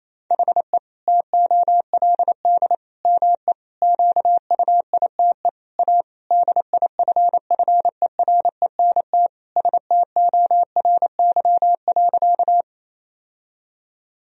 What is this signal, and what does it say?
Morse code, heard as a keyed tone, about 28 wpm, 700 Hz. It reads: HE TOLD ME QUITE A DIFFERENT STORY.
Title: Sentences from Top 1000 Words - Review Part 3 - 28wpm